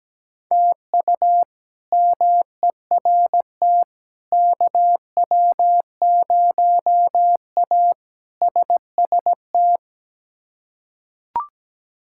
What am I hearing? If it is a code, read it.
Morse code, heard as a keyed tone, about 17 wpm, 700 Hz. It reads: TU MERT KW0A SST
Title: K1USN SST QSO Practice - 17wpm